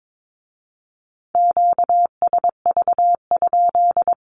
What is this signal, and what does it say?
Morse code, heard as a keyed tone, about 22 wpm, 700 Hz. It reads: QSV?
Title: Q-Codes with Spoken Meaning - 22wpm